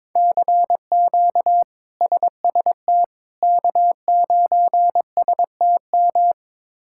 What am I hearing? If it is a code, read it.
Morse code, heard as a keyed tone, about 22 wpm, 700 Hz. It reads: CQ SST K9STM
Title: K1USN SST QSO Practice - 22wpm